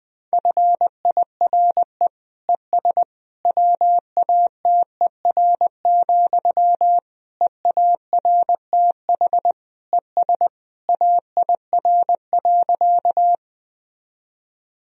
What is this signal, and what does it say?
Morse code, heard as a keyed tone, about 21 wpm, 700 Hz. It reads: FIRE ES WATER, EARTH ES AIR.
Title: Sentences from Top 400 Words and Abbreviations 20wpm